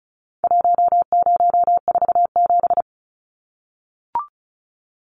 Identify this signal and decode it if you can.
Morse code, heard as a keyed tone, about 34 wpm, 700 Hz. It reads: 1047
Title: Numbers - Contest Style - 4-Digit 35wpm